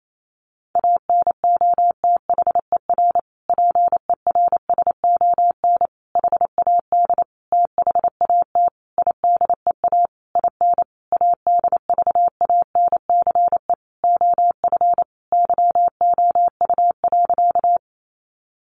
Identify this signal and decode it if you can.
Morse code, heard as a keyed tone, about 28 wpm, 700 Hz. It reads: ANOTHER PERSON HAD THAT IDEA IN ADVANCE OF YOU.
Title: Sentences from Top 1400 Words 28wpm